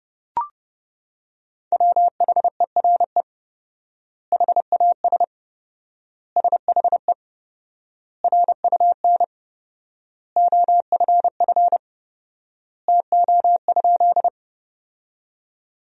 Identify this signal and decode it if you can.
Morse code, heard as a keyed tone, about 31 wpm, 700 Hz. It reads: WHERE HAS SHE RUN OFF TO?
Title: 4x Word-Spacing - Sentences from Top 300 Words - 30wpm